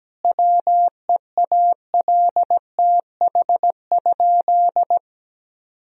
Morse code, heard as a keyed tone, about 17 wpm, 700 Hz. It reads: WEALTH?
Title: Sentences from Top 1000 Words - Review Part 4 - 17wpm